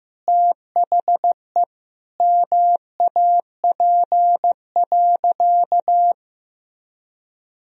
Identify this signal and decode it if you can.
Morse code, heard as a keyed tone, about 15 wpm, 700 Hz. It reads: THE MAP.
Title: Sentences from Top 1000 Words - Review Part 2 - 15wpm